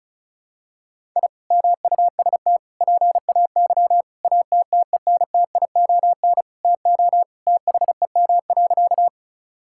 Morse code, heard as a keyed tone, about 36 wpm, 700 Hz. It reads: I MUST PAY ATTENTION TO THEM.
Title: Sentences from Top 1000 Words - Review Part 6 - 35wpm